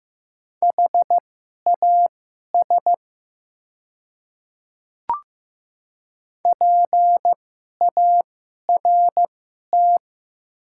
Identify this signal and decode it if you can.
Morse code, heard as a keyed tone, about 15 wpm, spaced as for 11 wpm, 700 Hz. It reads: HAS PART
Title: 2x Character Spacing - Top 500 Words - 15wpm